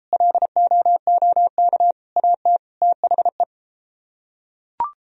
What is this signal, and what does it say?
Morse code, heard as a keyed tone, about 33 wpm, 700 Hz. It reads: LOOK AT THE